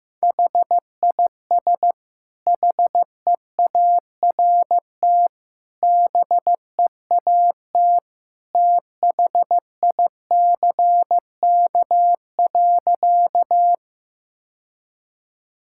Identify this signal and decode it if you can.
Morse code, heard as a keyed tone, about 15 wpm, 700 Hz. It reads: HIS HEART BEAT THICK.
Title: Sentences from Top 1600 Words 15wpm